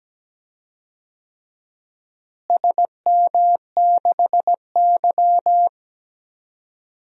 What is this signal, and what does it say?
Morse code, heard as a keyed tone, about 17 wpm, 700 Hz. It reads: SM6Y
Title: DXCC-100 Call Signs - 4 Characters - Repeated 3x - 17wpm